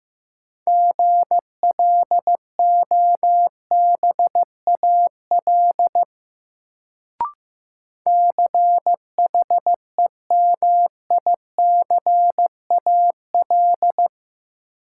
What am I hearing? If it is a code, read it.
Morse code, heard as a keyed tone, about 15 wpm, 700 Hz. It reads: GLOBAL CHEMICAL
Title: Sentences from Top 1200 Words - New Word Drill Alt - 15wpm